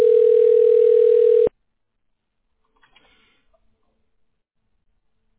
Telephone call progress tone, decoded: ringback tone